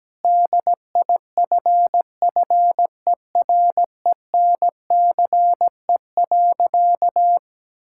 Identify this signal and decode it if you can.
Morse code, heard as a keyed tone, about 17 wpm, 700 Hz. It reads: DIFFERENCE.
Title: Sentences from Top 1000 Words - Review Part 5 - 17wpm